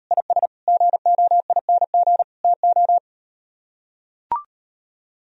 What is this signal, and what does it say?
Morse code, heard as a keyed tone, about 38 wpm, 700 Hz. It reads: IS GOING TO